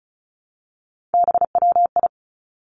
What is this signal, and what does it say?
Morse code, heard as a keyed tone, about 36 wpm, 700 Hz. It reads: BWI
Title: Three Challenging Letter-Number Combinations 35wpm